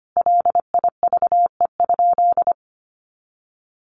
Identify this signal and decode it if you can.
Morse code, heard as a keyed tone, about 25 wpm, 700 Hz. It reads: LIVE?
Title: Sentences from Top 200 Words 25wpm